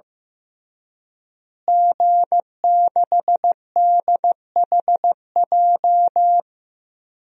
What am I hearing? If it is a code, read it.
Morse code, heard as a keyed tone, about 16 wpm, 700 Hz. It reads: G6DHJ
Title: DXCC-100 Call Signs - 5 Characters - Repeated 3x - 15wpm